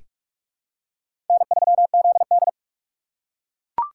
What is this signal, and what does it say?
Morse code, heard as a keyed tone, about 44 wpm, 700 Hz. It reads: N3ZD
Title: USA Call Signs - 4 Characters - 6x Character Spacing - 45wpm